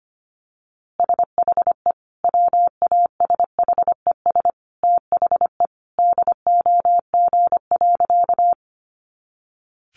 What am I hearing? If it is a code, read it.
Morse code, heard as a keyed tone, about 25 wpm, 700 Hz. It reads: SHE WASHES THE DOG.
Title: Easy Phrases 25wpm